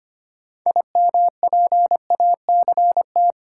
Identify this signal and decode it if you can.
Morse code, heard as a keyed tone, about 25 wpm, 700 Hz. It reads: IMPACT
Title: Sentences from Top 1000 Words - Review Part 6 - New Word Drill - 25wpm